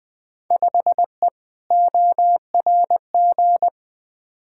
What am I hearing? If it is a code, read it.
Morse code, heard as a keyed tone, about 20 wpm, 700 Hz. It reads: 5E ORG
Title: Field Day Exchanges Encore - 20wpm